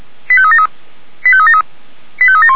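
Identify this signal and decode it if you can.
Five-tone selective call (ZVEI), heard as quick runs of five tones, tones of about 80 ms. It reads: 85373 85373 85371
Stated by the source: Five-tone ZVEI selective call received on an SDR.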